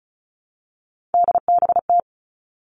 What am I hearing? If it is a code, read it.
Morse code, heard as a keyed tone, about 36 wpm, 700 Hz. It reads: DBT
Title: Three Challenging Letter-Number Combinations 35wpm